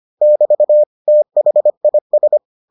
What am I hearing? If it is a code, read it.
Morse code, heard as a keyed tone, about 25 wpm, 600 Hz. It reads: <BT> THIS